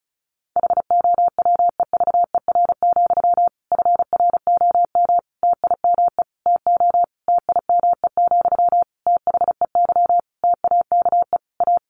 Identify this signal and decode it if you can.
Morse code, heard as a keyed tone, about 34 wpm, 700 Hz. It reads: HOWEVER, FROM TIME TO TIME, THEY TAKE A
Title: Sentences from Top 1300 Words 35wpm